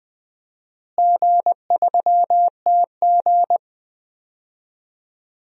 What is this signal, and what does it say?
Morse code, heard as a keyed tone, about 21 wpm, 700 Hz. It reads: G3TG
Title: DXCC-100 Call Signs - 4 Characters - Repeated 2x - 20wpm